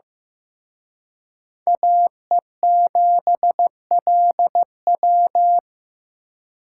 Morse code, heard as a keyed tone, about 18 wpm, 700 Hz. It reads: AE7LW
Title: Speed Racing - USA 2x2 Call Signs 15wpm